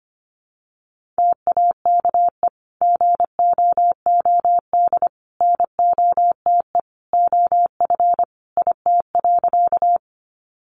Morse code, heard as a keyed tone, about 25 wpm, 700 Hz. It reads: TAKE GOOD NOTE OF IT.
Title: Sentences from Top 500 Words 25wpm